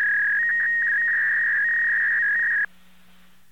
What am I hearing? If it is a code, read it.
Fax machine handshake.